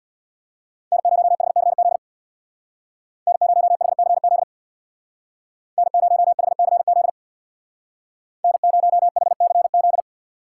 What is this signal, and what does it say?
Morse code, heard as a keyed tone, about 64 wpm, 700 Hz. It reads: N0HCZ N0HCZ N0HCZ N0HCZ